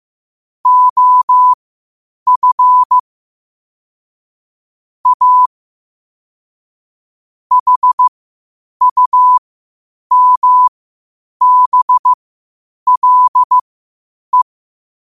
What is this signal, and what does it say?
Morse code, heard as a keyed tone, about 15 wpm, spaced as for 8 wpm, 1000 Hz. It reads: OF A HUMBLE
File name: qotdfiles / 8wpm